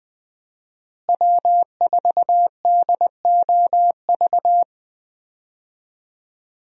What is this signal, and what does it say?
Morse code, heard as a keyed tone, about 20 wpm, 700 Hz. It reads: W4DOV
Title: USA Call Signs - 5 Characters - Repeated 3x - 20wpm